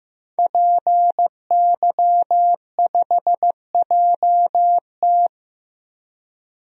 Morse code, heard as a keyed tone, about 15 wpm, 700 Hz. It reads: PY5JT
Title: DXCC-100 Call Signs - 5 Characters - 15wpm